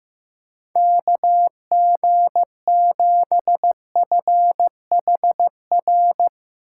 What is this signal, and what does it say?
Morse code, heard as a keyed tone, about 15 wpm, 700 Hz. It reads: KG7FHR
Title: USA Call Signs - 6 Characters - 8x Character Spacing - 15wpm